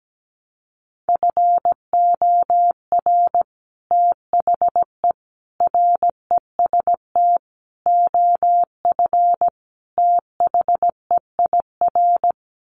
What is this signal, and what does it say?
Morse code, heard as a keyed tone, about 17 wpm, 700 Hz. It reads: FOR THE REST OF THEIR
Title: Sets of 5 Words Encore2 17wpm